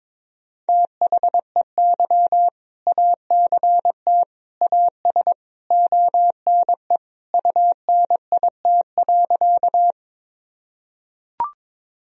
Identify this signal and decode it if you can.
Morse code, heard as a keyed tone, about 21 wpm, 700 Hz. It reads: THEY ACT AS ONE UNIT.
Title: Sentences from Top 1000 Words - Review Part 2 - 22wpm